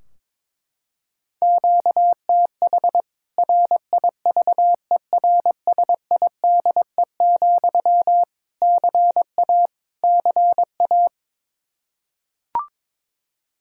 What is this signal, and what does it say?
Morse code, heard as a keyed tone, about 22 wpm, 700 Hz. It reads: QTH RIVERSIDE, CA CA
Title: K1USN SST QSO Practice 22wpm